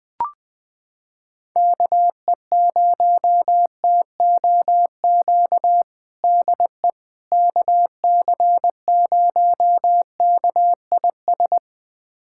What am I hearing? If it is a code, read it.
Morse code, heard as a keyed tone, about 21 wpm, 700 Hz. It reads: KE0TOQ DE KC0KIS
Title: USA Callsign DE Callsign - 6 Characters - 20wpm